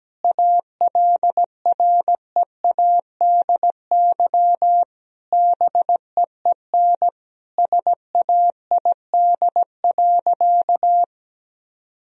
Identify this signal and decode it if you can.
Morse code, heard as a keyed tone, about 17 wpm, 700 Hz. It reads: ALREADY BEEN SAID.
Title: Sentences from Top 1000 Words - Review Part 1 - 17wpm